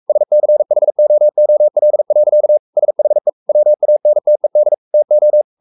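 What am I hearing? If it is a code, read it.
Morse code, heard as a keyed tone, about 44 wpm, 600 Hz. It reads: SCHOOL. SHE WANTED TO
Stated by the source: Morse code at 43 wpm sending Ken Follett's novel Never in English, from CW med Gnister.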